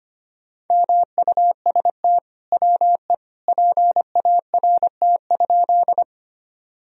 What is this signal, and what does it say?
Morse code, heard as a keyed tone, about 25 wpm, 700 Hz. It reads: MUST WE PART?
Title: Sentences from Top 1000 Words - Review Part 1 - 25wpm